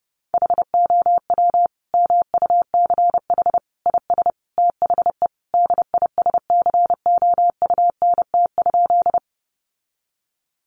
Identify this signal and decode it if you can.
Morse code, heard as a keyed tone, about 30 wpm, 700 Hz. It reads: HOW MUCH IS THE DISCOUNT?
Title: Sentences from Top 1200 Words 30wpm